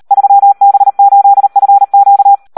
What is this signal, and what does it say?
Morse code, heard as a keyed tone, about 38 wpm, 800 Hz. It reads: 3B8FQ
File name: Audio practica CW / SD 1 38